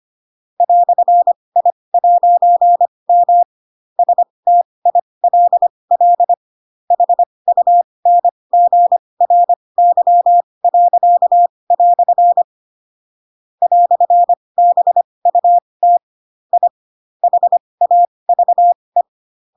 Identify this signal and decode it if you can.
Morse code, heard as a keyed tone, about 25 wpm, spaced as for 20 wpm, 700 Hz. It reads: "I'M STILL HUNGRY." "BUT I HAVE
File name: Pinocchio / Pinocchio ch7 15 25 ik1hge